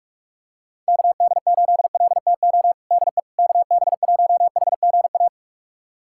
Morse code, heard as a keyed tone, about 45 wpm, 700 Hz. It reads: KD8LTO DE KB1HGA
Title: USA Callsign DE Callsign - 6 Characters - 45wpm